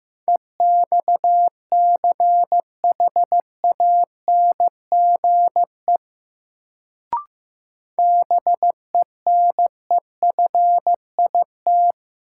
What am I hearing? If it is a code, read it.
Morse code, heard as a keyed tone, about 15 wpm, 700 Hz. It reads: EXCHANGE BENEFIT
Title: Sentences from Top 1000 Words - Review Part 6 - New Word Drill - 15wpm